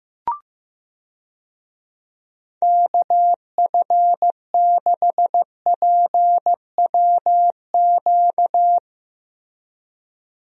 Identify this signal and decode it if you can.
Morse code, heard as a keyed tone, about 15 wpm, 700 Hz. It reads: KF6PWQ
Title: USA Call Signs - 6 Characters - Repeated 3x - 15wpm